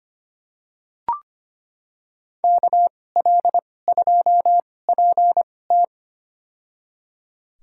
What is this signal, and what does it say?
Morse code, heard as a keyed tone, about 25 wpm, spaced as for 18 wpm, 700 Hz. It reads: KL2PT
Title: USA Call Signs - 5 Characters - 2x Character Spacing - 25wpm